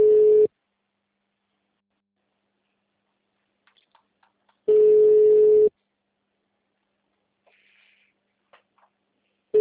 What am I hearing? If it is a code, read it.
Ringback tone (Europe, 425 Hz).